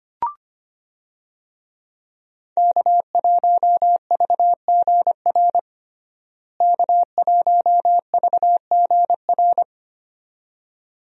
Morse code, heard as a keyed tone, about 25 wpm, 700 Hz. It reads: K1VGR K1VGR
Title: USA Call Signs - 5 Characters - Repeated 2x - 25wpm